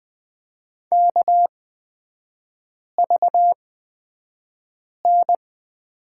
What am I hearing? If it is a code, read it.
Morse code, heard as a keyed tone, about 20 wpm, spaced as for 5 wpm, 700 Hz. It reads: KVN
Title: 243 Lesson 39e - Letter z - Three Characters 20wpm